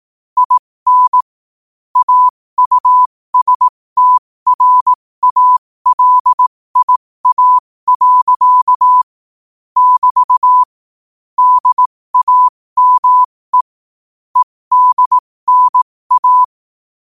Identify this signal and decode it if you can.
Morse code, heard as a keyed tone, about 18 wpm, spaced as for 15 wpm, 1000 Hz. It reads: IN AUSTRALIA. <BT> DAME EDNA
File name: qotdfiles / 15wpm